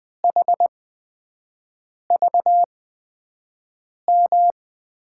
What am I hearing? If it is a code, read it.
Morse code, heard as a keyed tone, about 20 wpm, spaced as for 5 wpm, 700 Hz. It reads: HVM
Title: USA Call Signs - 6 Characters - 8x Character Spacing - 20wpm